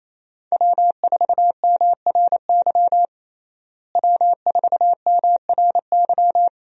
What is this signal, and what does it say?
Morse code, heard as a keyed tone, about 28 wpm, 700 Hz. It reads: W4MRY W4MRY
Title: USA Call Signs - 5 Characters - Repeated 2x - 28wpm